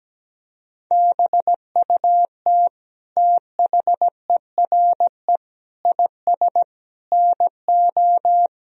Morse code, heard as a keyed tone, about 17 wpm, 700 Hz. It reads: BUT THERE IS NO
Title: Sentences from Top 1000 Words - Review Part 3 - 17wpm